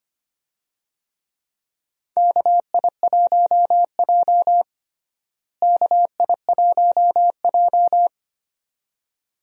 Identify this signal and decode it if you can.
Morse code, heard as a keyed tone, about 25 wpm, 700 Hz. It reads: KI1J KI1J
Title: USA Call Signs - 4 Characters - Repeated 2x - 25wpm